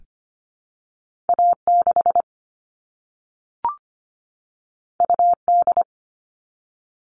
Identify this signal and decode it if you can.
Morse code, heard as a keyed tone, about 25 wpm, 700 Hz. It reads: A6 UD
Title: All Cut Numbers - 2-Digit 25wpm